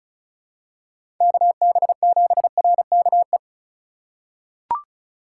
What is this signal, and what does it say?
Morse code, heard as a keyed tone, about 36 wpm, 700 Hz. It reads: KB7RKE
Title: USA Call Signs - 2x Character Spacing - 35wpm